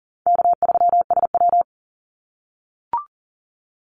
Morse code, heard as a keyed tone, about 37 wpm, 700 Hz. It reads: K3SW
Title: Easy Call Signs - 40wpm